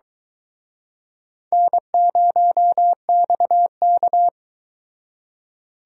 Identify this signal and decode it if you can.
Morse code, heard as a keyed tone, about 23 wpm, 700 Hz. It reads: N0XK